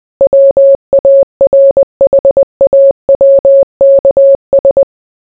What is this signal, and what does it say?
Morse code, heard as a keyed tone, about 20 wpm, 550 Hz. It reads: WARHAWKS